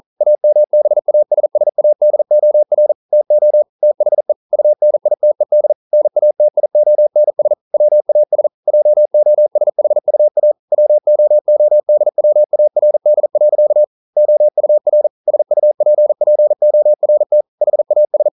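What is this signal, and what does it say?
Morse code, heard as a keyed tone, about 42 wpm, 600 Hz. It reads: AMBASSADOR TO THE UNITED NATIONS WAS JOSHUA WOODWARD. OUR SUPPORT HAS
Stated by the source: Morse code practice at 41 wpm, from CW med Gnister.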